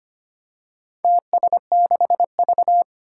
Morse code, heard as a keyed tone, about 25 wpm, 700 Hz. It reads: TS6V
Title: Four Challenging Letter-Number Combinations 25wpm